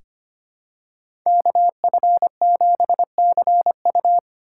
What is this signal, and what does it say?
Morse code, heard as a keyed tone, about 25 wpm, 700 Hz. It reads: KF7CU
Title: USA Call Signs - 5 Characters - 8x Character Spacing - 25wpm